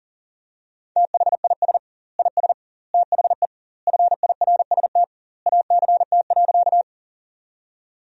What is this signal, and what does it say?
Morse code, heard as a keyed tone, about 37 wpm, 700 Hz. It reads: THIS IS THE FIRST ACT.
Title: 2x Word-Spacing - Sentences from Top 200 Words - 40wpm